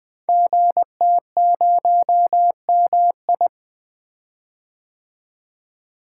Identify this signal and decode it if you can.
Morse code, heard as a keyed tone, about 20 wpm, 700 Hz. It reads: GT0MI
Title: Code Groups - 5 Letter:Numbers - 20wpm